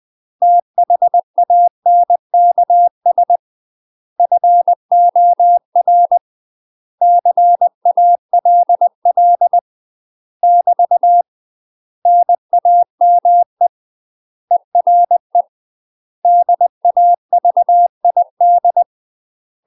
Morse code, heard as a keyed tone, about 20 wpm, 700 Hz. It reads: THANKS FOR CALL <BT> NAME ERE DAVID
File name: UK MASTER QSOs TRAINING / 20WPM / UK MASTER QSOs 0035